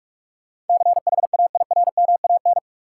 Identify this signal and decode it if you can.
Morse code, heard as a keyed tone, about 43 wpm, 700 Hz. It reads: CHAIRMAN
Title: Sets of 3 Words - New Word Drill - 45wpm